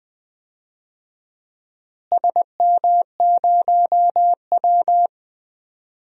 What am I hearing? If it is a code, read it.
Morse code, heard as a keyed tone, about 20 wpm, 700 Hz. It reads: SM0W
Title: DXCC-100 Call Signs - 4 Characters - Repeated 3x - 20wpm